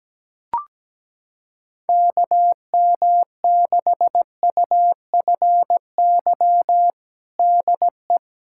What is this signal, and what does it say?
Morse code, heard as a keyed tone, about 17 wpm, 700 Hz. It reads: KM6UFY DE
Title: USA Callsign DE Callsign 17wpm